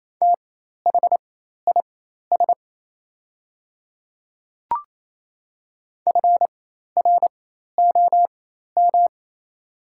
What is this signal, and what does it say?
Morse code, heard as a keyed tone, about 28 wpm, spaced as for 13 wpm, 700 Hz. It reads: THIS FROM
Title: 4x Character Spacing - Top 500 Words - 28wpm